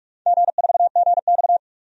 Morse code, heard as a keyed tone, about 42 wpm, 700 Hz. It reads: G4GX
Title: DXCC-100 Call Signs - 4 Characters - 45wpm